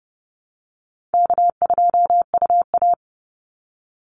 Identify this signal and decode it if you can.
Morse code, heard as a keyed tone, about 30 wpm, 700 Hz. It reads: K2UA
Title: Easy Call Signs Repeated 3x - 30wpm